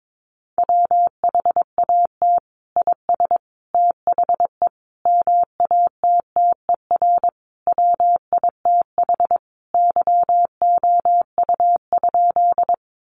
Morse code, heard as a keyed tone, about 22 wpm, 700 Hz. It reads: WHAT IS THE MATTER WITH YOU?
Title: Sentences from Top 700 Words 22wpm